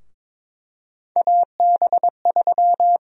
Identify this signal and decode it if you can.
Morse code, heard as a keyed tone, about 22 wpm, 700 Hz. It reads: AB3
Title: USA Call Sign Prefixes - 22wpm